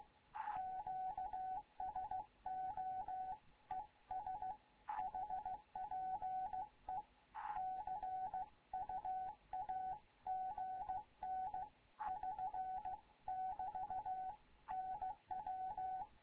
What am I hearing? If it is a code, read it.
Morse code, heard as a keyed tone, about 16 wpm, 750 Hz, with radio noise. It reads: QSO ES HPE CUAGN <SN> <BT> NW
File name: Morze-HA2MN / hangp / 2 11 GYAKORLAT